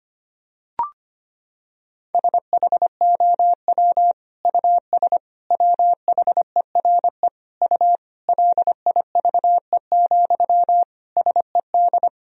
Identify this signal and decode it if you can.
Morse code, heard as a keyed tone, about 25 wpm, 700 Hz. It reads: SHOW US WHERE U LIVE, SED
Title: Sentences from Top 200 Words and Abbreviations 25wpm